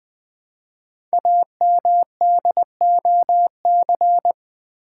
Morse code, heard as a keyed tone, about 20 wpm, 700 Hz. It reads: AMDOC
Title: Code Groups - 5 Letters - 20wpm